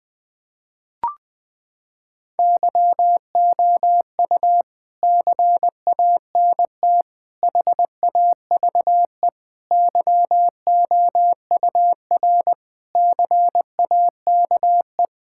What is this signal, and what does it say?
Morse code, heard as a keyed tone, about 21 wpm, 700 Hz. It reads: YOU CANT HAVE YOUR CAKE
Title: Sentences with 9-Word Phrases - 20wpm